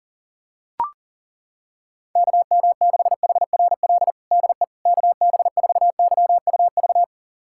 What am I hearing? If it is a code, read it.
Morse code, heard as a keyed tone, about 37 wpm, 700 Hz. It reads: KM6HRL DE KB4YUV
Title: USA Callsign DE Callsign 40wpm